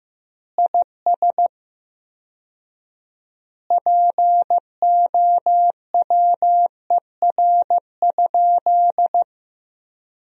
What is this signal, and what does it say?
Morse code, heard as a keyed tone, about 15 wpm, 700 Hz. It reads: IS POWER?
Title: 4x Word-Spacing - Sentences from Top 500 Words - 15wpm